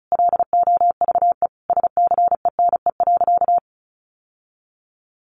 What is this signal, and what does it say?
Morse code, heard as a keyed tone, about 34 wpm, 700 Hz. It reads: LOVE SCENE.
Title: Sentences from Top 1300 Words 35wpm